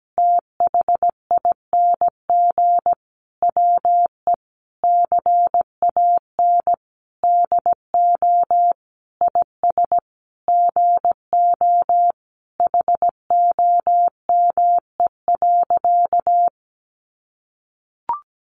Morse code, heard as a keyed tone, about 17 wpm, 700 Hz. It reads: THING WE CAN DO IS GO HOME.
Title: Sentences from Top 200 Words 17wpm